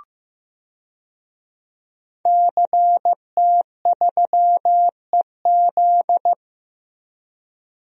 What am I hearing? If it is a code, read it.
Morse code, heard as a keyed tone, about 16 wpm, 700 Hz. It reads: CT3EZ
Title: DXCC-100 Call Signs - 5 Characters - Repeated 3x - 15wpm